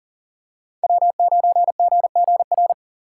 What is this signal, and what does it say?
Morse code, heard as a keyed tone, about 40 wpm, 700 Hz. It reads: W9GGR
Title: USA Call Signs - 5 Characters - 8x Character Spacing - 40wpm